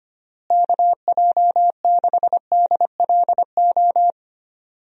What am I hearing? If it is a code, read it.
Morse code, heard as a keyed tone, about 25 wpm, 700 Hz. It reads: KJ6DLO
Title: USA Call Signs - 6 Characters - Repeated 3x - 25wpm